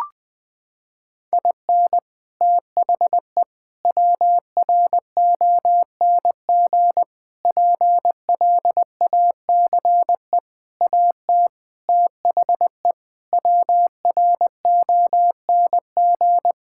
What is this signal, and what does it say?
Morse code, heard as a keyed tone, about 20 wpm, 700 Hz. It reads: IN THE WRONG PLACE AT THE WRONG
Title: Sets of 8 Words - 20wpm